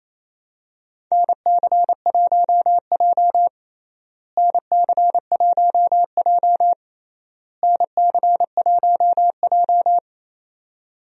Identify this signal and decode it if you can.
Morse code, heard as a keyed tone, about 28 wpm, 700 Hz. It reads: NC1J NC1J NC1J
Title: USA Call Signs - 4 Characters - Repeated 3x - 28wpm